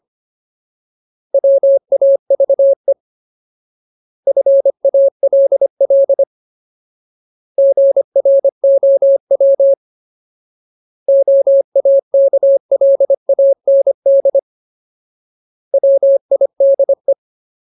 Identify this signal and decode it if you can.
Morse code, heard as a keyed tone, about 25 wpm, 550 Hz. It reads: WAVE FALL GROW OAKLAND WIDE
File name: Ultimate Word List / 25WPM Ultimate